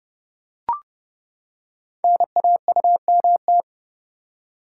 Morse code, heard as a keyed tone, about 30 wpm, 700 Hz. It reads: NAUMT
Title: Code Groups - 5 Letters - 30wpm